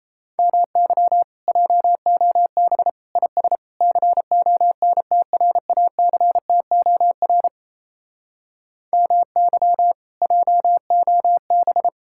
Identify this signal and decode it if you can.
Morse code, heard as a keyed tone, about 37 wpm, 700 Hz. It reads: MY JOB IS CONTRACTOR MY JOB